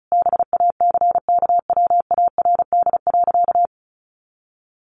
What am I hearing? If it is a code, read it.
Morse code, heard as a keyed tone, about 34 wpm, 700 Hz. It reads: BACKWARD.
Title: A Sound Of Thunder 35wpm